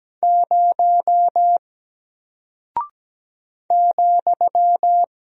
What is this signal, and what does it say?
Morse code, heard as a keyed tone, about 17 wpm, 700 Hz. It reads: 0 ,
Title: Single Character (Letter:Number:Punctuation:Prosign) Alt 17wpm